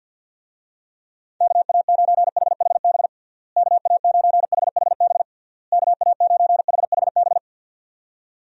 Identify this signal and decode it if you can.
Morse code, heard as a keyed tone, about 47 wpm, 700 Hz. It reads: KA9HHB KA9HHB KA9HHB
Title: USA Call Signs - 6 Characters - Repeated 3x - 50wpm